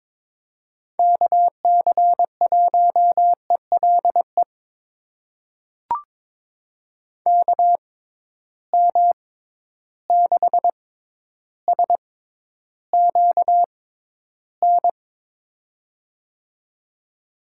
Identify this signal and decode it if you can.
Morse code, heard as a keyed tone, about 22 wpm, spaced as for 11 wpm, 700 Hz. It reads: KC1ELE KM6SQN
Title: USA Call Signs - 6 Characters - 6x Character Spacing - 22wpm